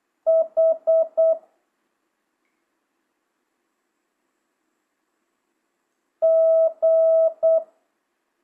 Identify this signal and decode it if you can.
Morse code, heard as a keyed tone, about 8 wpm, 1300 Hz. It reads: H G
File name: jh8cbh / 2025cw